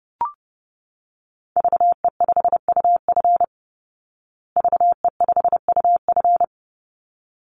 Morse code, heard as a keyed tone, about 30 wpm, 700 Hz. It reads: VE5UF VE5UF
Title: Call Signs Repeated 3x - 30wpm